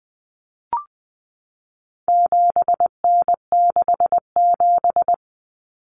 Morse code, heard as a keyed tone, about 20 wpm, 700 Hz. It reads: 7N67
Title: Common Cut Numbers - 4-Digit 20wpm